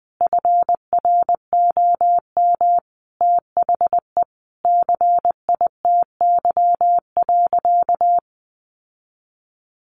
Morse code, heard as a keyed tone, about 20 wpm, 700 Hz. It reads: FROM THE CITY.
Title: Sentences from Top 1200 Words 20wpm